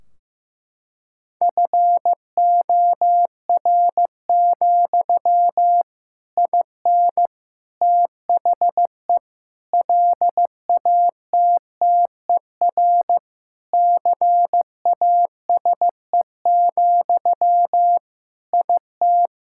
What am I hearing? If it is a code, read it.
Morse code, heard as a keyed tone, about 15 wpm, 700 Hz. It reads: FOR, IN THE LATTER CASE, IT
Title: Sentences from Top 1600 Words 15wpm